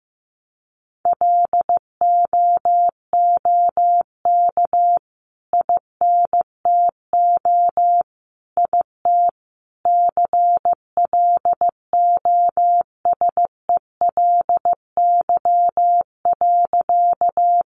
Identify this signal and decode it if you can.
Morse code, heard as a keyed tone, about 14 wpm, 700 Hz. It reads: LOOK INTO IT CLOSELY.
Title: Sentences from Top 1400 Words 15wpm